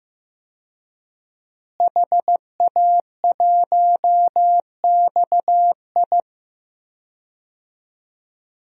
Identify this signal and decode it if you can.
Morse code, heard as a keyed tone, about 15 wpm, 700 Hz. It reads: HA1XI
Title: DXCC-100 Call Signs - 5 Characters - Repeated 3x - 15wpm